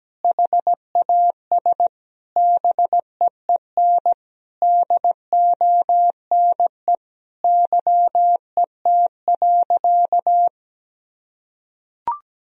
Morse code, heard as a keyed tone, about 17 wpm, 700 Hz. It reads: HAS BEEN DONE YET.
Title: Sentences from Top 1000 Words - Review Part 2 - 17wpm